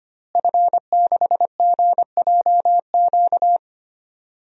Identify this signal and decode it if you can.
Morse code, heard as a keyed tone, about 25 wpm, 700 Hz. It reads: F6GJQ
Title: DXCC-100 Call Signs - 5 Characters - Repeated 2x - 25wpm